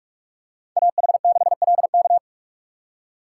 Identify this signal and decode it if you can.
Morse code, heard as a keyed tone, about 43 wpm, 700 Hz. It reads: AH6LK